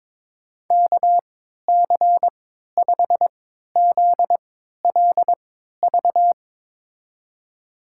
Morse code, heard as a keyed tone, about 22 wpm, spaced as for 13 wpm, 700 Hz. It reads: KC5ZLV
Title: USA Call Signs - 6 Characters - 3x Character Spacing - 22wpm